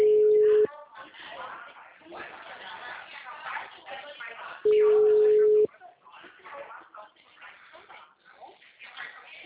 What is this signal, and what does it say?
Ringback tone (Europe, 425 Hz), with speech.